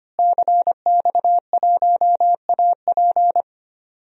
Morse code, heard as a keyed tone, about 25 wpm, 700 Hz. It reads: CX1AP
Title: DXCC-100 Call Signs - 5 Characters - Repeated 2x - 25wpm